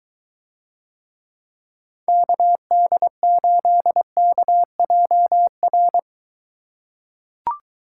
Morse code, heard as a keyed tone, about 22 wpm, 700 Hz. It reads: KD8KJR